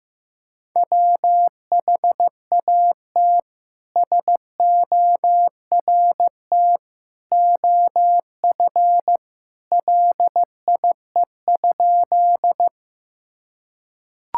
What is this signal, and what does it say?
Morse code, heard as a keyed tone, about 15 wpm, 700 Hz. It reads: WHAT SORT OF LIE?
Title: Sentences from Top 1600 Words 15wpm